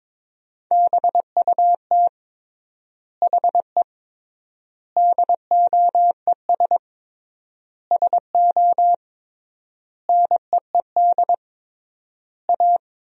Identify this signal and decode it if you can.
Morse code, heard as a keyed tone, about 22 wpm, 700 Hz. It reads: BUT HE DOES SO NEED A
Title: 3x Word-Spacing - Sentences from Top 200 Words - 22wpm